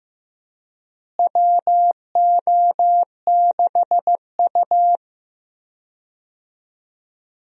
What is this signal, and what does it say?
Morse code, heard as a keyed tone, about 15 wpm, 700 Hz. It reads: WO6U
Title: USA Call Signs - 4 Characters - Repeated 3x - 15wpm